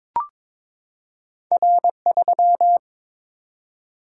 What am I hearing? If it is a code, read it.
Morse code, heard as a keyed tone, about 22 wpm, 700 Hz. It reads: R3
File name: practice / 22wpm / Two Letter-Number Combinations - 22wpm